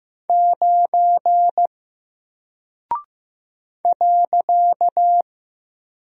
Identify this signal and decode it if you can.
Morse code, heard as a keyed tone, about 15 wpm, 700 Hz. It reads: 9 .
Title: Single Character (Letter:Number:Punctuation:Prosign) 15wpm